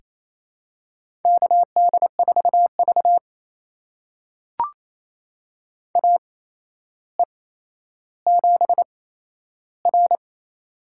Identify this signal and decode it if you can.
Morse code, heard as a keyed tone, about 28 wpm, spaced as for 12 wpm, 700 Hz. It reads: KD4V AE7R
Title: USA Call Signs - 4 Characters - 8x Character Spacing - 28wpm